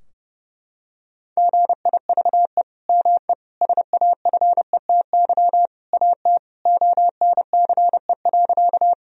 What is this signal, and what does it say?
Morse code, heard as a keyed tone, about 30 wpm, 700 Hz. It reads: GIVE ME SAFETY AT ONCE.
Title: Sentences from Top 1000 Words - Review Part 4 - 30wpm